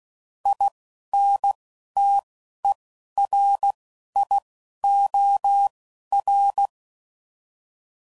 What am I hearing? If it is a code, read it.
Morse code, heard as a keyed tone, about 16 wpm, 800 Hz. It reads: I N T E R I O R
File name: Prefixes / By First Letter / Is